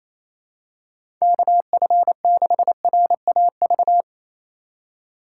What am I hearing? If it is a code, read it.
Morse code, heard as a keyed tone, about 28 wpm, 700 Hz. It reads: KF6RAV
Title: USA Call Signs - 6 Characters - 4x Character Spacing - 28wpm